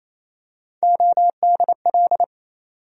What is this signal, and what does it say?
Morse code, heard as a keyed tone, about 27 wpm, 700 Hz. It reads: ODL
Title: USA Call Sign Postfixes - 3 Letters 28wpm